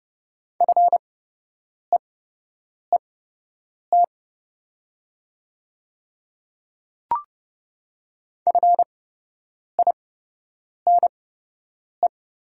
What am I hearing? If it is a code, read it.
Morse code, heard as a keyed tone, about 29 wpm, spaced as for 8 wpm, 700 Hz. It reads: FEET FINE
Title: 8x Character Spacing - Top 500 Words - 30wpm